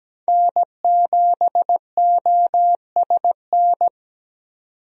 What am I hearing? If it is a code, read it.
Morse code, heard as a keyed tone, about 17 wpm, 700 Hz. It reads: N7OSN
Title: USA Call Signs - 5 Characters - 17wpm